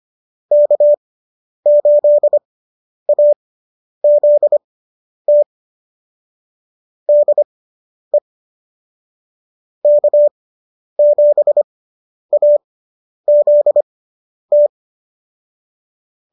Morse code, heard as a keyed tone, about 25 wpm, spaced as for 10 wpm, 590 Hz. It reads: K8AZT DE K7AZT